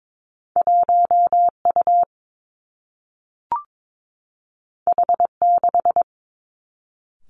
Morse code, heard as a keyed tone, about 22 wpm, 700 Hz. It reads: 1 U H 6
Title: Two Challenging Letter-Number Combinations 22wpm